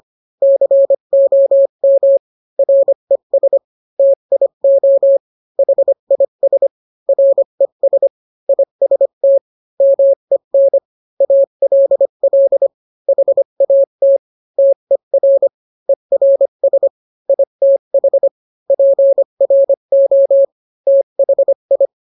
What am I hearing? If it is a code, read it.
Morse code, heard as a keyed tone, about 25 wpm, 550 Hz. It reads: COM RES TIO HIS RES IST MEN ALL HAT TER ERS ITH PRO THI